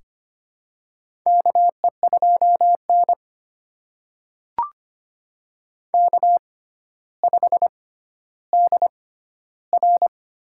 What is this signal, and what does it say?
Morse code, heard as a keyed tone, about 25 wpm, spaced as for 13 wpm, 700 Hz. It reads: KE2N K5DR
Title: USA Call Signs - 4 Characters - 6x Character Spacing - 25wpm